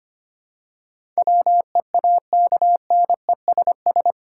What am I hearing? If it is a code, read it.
Morse code, heard as a keyed tone, about 25 wpm, 690 Hz. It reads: WEAKNESS
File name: 8 Letter Words - First 3 Letters Given - 25wpm